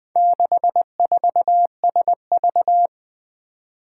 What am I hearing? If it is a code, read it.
Morse code, heard as a keyed tone, about 20 wpm, 700 Hz. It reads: 64SV
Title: Four Challenging Letter-Number Combinations 20wpm